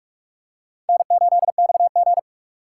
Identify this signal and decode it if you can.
Morse code, heard as a keyed tone, about 41 wpm, 700 Hz. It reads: N8XG
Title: USA Call Signs - 6x Character Spacing - 45wpm